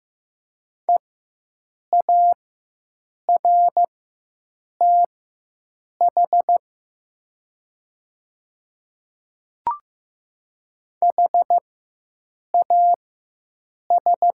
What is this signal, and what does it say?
Morse code, heard as a keyed tone, about 15 wpm, spaced as for 7 wpm, 700 Hz. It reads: EARTH HAS
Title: 4x Character Spacing - Top 300 Words - 15wpm